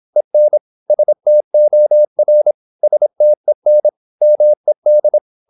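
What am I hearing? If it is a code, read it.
Morse code, heard as a keyed tone, about 26 wpm, 600 Hz. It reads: EN STOR STEN MED
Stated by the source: High-speed (QRQ) Morse at 26 wpm.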